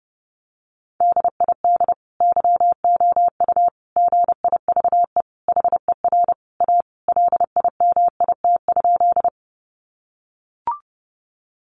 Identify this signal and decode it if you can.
Morse code, heard as a keyed tone, about 30 wpm, 700 Hz. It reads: DID YOU GIVE HER A LIMIT?
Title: Sentences from Top 1500 Words 30wpm